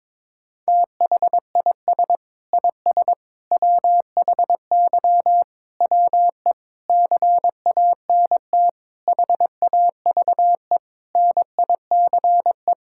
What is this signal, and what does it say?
Morse code, heard as a keyed tone, about 22 wpm, 700 Hz. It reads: THIS IS WHY WE CANT HAVE NICE
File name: practice / 22wpm / Sets of 8 Words - 22wpm